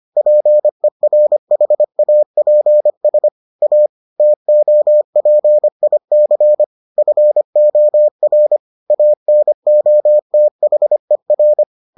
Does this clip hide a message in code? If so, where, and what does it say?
Yes: Morse code from 0.2 to 11.6 s, heard as a keyed tone, about 25 wpm, 600 Hz. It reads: PERHAPS A TOPIC FOR ANOTHER